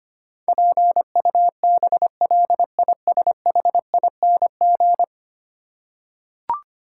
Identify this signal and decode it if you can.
Morse code, heard as a keyed tone, about 25 wpm, 700 Hz. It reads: PUBLISHING
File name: Ten Letter Words - 25wpm